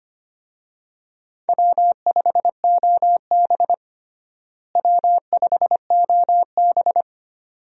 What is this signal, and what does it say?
Morse code, heard as a keyed tone, about 25 wpm, 700 Hz. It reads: W5OB W5OB
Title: USA Call Signs - 4 Characters - Repeated 2x - 25wpm